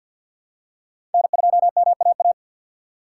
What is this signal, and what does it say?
Morse code, heard as a keyed tone, about 52 wpm, 700 Hz. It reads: N2MAA
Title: USA Call Signs - 5 Characters - 3x Character Spacing - 50wpm